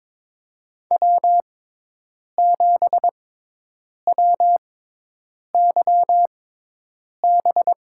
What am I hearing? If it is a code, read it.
Morse code, heard as a keyed tone, about 22 wpm, spaced as for 8 wpm, 700 Hz. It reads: W7WYB
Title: USA Call Signs - 5 Characters - 6x Character Spacing - 22wpm